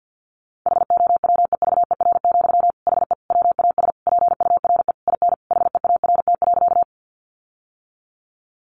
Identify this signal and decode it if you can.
Morse code, heard as a keyed tone, about 49 wpm, 700 Hz. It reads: HOWEVER, HE WAS PURE IN HEART.
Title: Sentences from Top 1500 Words 50wpm